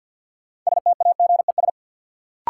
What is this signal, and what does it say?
Morse code, heard as a keyed tone, about 46 wpm, 700 Hz. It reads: STAGES
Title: Sets of 5 Words Encore - New Word Drill - 50wpm